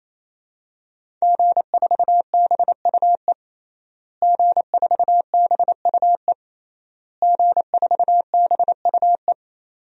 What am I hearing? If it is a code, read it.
Morse code, heard as a keyed tone, about 28 wpm, 700 Hz. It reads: G4BUE G4BUE G4BUE
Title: DXCC-100 Call Signs - 5 Characters - Repeated 3x - 28wpm